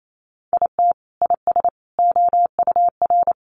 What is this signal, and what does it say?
Morse code, heard as a keyed tone, about 28 wpm, 700 Hz. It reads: IT IS OUR
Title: Sentences from Top 1200 Words 28wpm